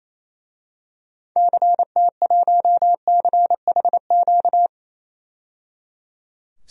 Morse code, heard as a keyed tone, about 28 wpm, 700 Hz. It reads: CT1CHQ
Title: DXCC-100 Call Signs - 6 Characters - 28wpm